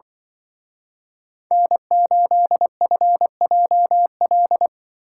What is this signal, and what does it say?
Morse code, heard as a keyed tone, about 25 wpm, 700 Hz. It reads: N8FJL
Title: Speed Racing - USA 1x3 Call Signs 20wpm